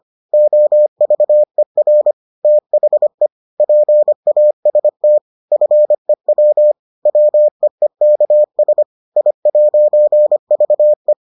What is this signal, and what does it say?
Morse code, heard as a keyed tone, about 25 wpm, 600 Hz. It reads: OVER THE PAST FEW WEEKS I'VE